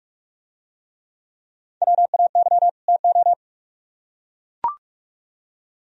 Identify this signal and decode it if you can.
Morse code, heard as a keyed tone, about 43 wpm, 700 Hz. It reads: WAY TO